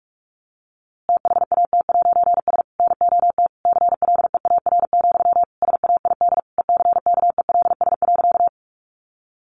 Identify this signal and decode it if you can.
Morse code, heard as a keyed tone, about 44 wpm, 700 Hz. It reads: THAT'S NOT CLEAR, SAID ECKELS.
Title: A Sound Of Thunder 45wpm